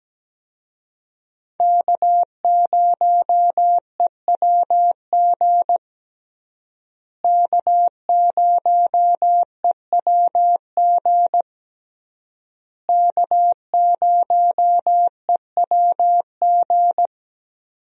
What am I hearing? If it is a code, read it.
Morse code, heard as a keyed tone, about 17 wpm, 700 Hz. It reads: K0EWG K0EWG K0EWG
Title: USA Call Signs - 5 Characters - Repeated 3x - 17wpm